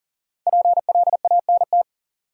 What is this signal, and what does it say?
Morse code, heard as a keyed tone, about 40 wpm, 700 Hz. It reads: PLANT
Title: Sentences from Top 1000 Words - Review Part 1 - New Word Drill - 40wpm